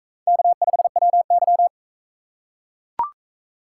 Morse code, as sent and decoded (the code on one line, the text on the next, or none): -.- ..... .-- -.--
K5WY